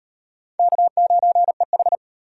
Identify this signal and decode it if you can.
Morse code, heard as a keyed tone, about 33 wpm, 700 Hz. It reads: K9EH